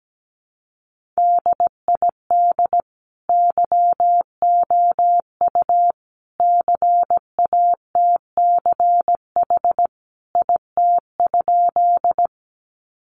Morse code, heard as a keyed tone, about 17 wpm, 700 Hz. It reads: DID YOU CATCH IT?
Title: Sentences from Top 1500 Words 17wpm